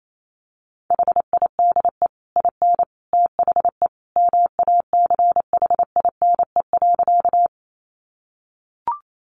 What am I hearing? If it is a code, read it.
Morse code, heard as a keyed tone, about 28 wpm, 700 Hz. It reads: HIDE IN THE MACHINE.
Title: A Sound Of Thunder 28wpm